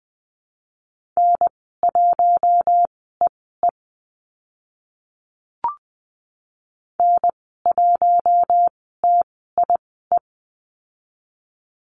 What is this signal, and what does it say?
Morse code, heard as a keyed tone, about 20 wpm, spaced as for 14 wpm, 700 Hz. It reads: N1EE N1TIE
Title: 042 Lesson 8g - Number 1 - Call Signs 20wpm